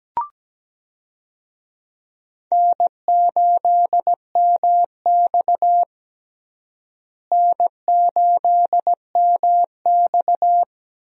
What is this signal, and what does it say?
Morse code, heard as a keyed tone, about 17 wpm, 700 Hz. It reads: N8MX N8MX
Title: USA Call Signs - 4 Characters - Repeated 2x - 17wpm